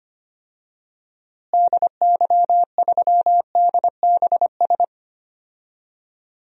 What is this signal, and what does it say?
Morse code, heard as a keyed tone, about 25 wpm, 700 Hz. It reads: DY3DBS
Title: DXCC-100 Call Signs - 6 Characters - Repeated 2x - 25wpm